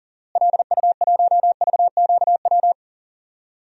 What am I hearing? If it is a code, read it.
Morse code, heard as a keyed tone, about 37 wpm, 700 Hz. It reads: LU1VQW
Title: DXCC-100 Call Signs - 6 Characters - Repeated 2x - 40wpm